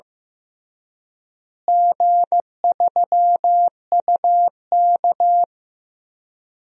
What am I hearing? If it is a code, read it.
Morse code, heard as a keyed tone, about 16 wpm, 700 Hz. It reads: G3UK
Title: DXCC-100 Call Signs - 4 Characters - Repeated 3x - 15wpm